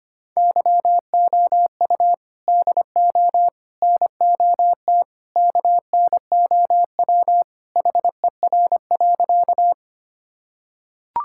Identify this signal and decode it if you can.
Morse code, heard as a keyed tone, about 25 wpm, 700 Hz. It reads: YOU DO NOT KNOW HER.
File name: practice / 25wpm / Sentences with 4-Word Phrases - 25wpm